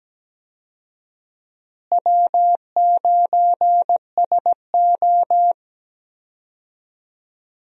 Morse code, heard as a keyed tone, about 17 wpm, 700 Hz. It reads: W9SO
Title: USA Call Signs - 4 Characters - Repeated 3x - 17wpm